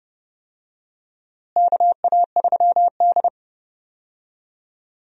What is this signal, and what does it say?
Morse code, heard as a keyed tone, about 30 wpm, 700 Hz. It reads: KA3D
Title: USA Call Signs - 4 Characters - Repeated 2x - 30wpm